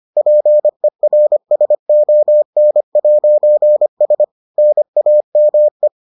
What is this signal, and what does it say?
Morse code, heard as a keyed tone, about 25 wpm, 600 Hz. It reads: PERSON'S NAME